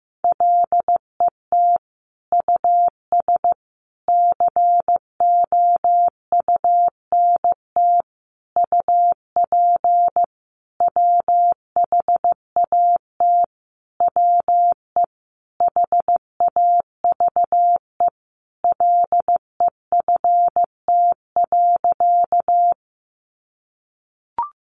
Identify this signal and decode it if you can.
Morse code, heard as a keyed tone, about 15 wpm, 700 Hz. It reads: LET US COUNT UP WHAT WE HAVE LEFT.
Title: Sentences from Top 1500 Words 15wpm